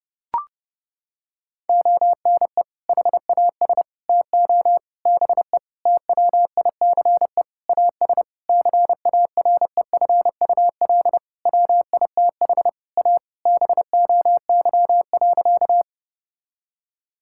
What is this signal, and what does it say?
Morse code, heard as a keyed tone, about 32 wpm, 700 Hz. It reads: ONE HAS TO BE TWICE AS CAREFUL WITH A BOY.
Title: Sentences from Top 1600 Words 30wpm